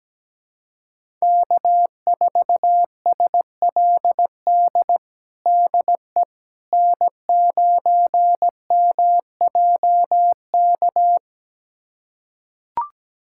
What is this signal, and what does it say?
Morse code, heard as a keyed tone, about 17 wpm, 700 Hz. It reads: K4SLD DE N9MJK
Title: USA Callsign DE Callsign - 5 Characters - 17wpm